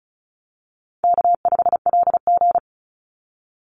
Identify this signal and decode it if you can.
Morse code, heard as a keyed tone, about 34 wpm, 700 Hz. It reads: K5LG
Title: Easy Call Signs Repeated 3x - 35wpm